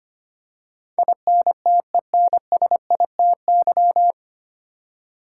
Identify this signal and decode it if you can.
Morse code, heard as a keyed tone, about 25 wpm, 700 Hz. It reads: INTENSITY
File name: Nine Letter Words - 25wpm